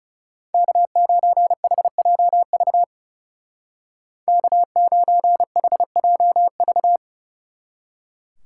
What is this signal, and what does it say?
Morse code, heard as a keyed tone, about 36 wpm, 700 Hz. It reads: K9HJV K9HJV
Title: Speed Racing - USA 1x3 Call Signs 30wpm